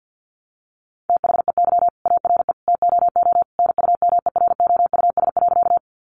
Morse code, heard as a keyed tone, about 50 wpm, 700 Hz. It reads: THEY ARE TOO NUMEROUS.
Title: Sentences from Top 1400 Words 50wpm